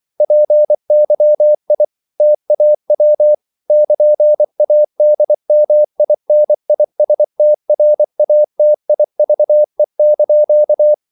Morse code, heard as a keyed tone, about 24 wpm, 600 Hz. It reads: PYI TAW <KN>ADMINISTRATIVE)
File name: CW / Practice / NationalCapitals24